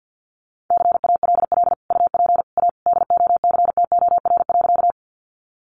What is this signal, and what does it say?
Morse code, heard as a keyed tone, about 50 wpm, 700 Hz. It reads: CALL UP A DOCTOR.
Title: Sentences from Top 1500 Words 50wpm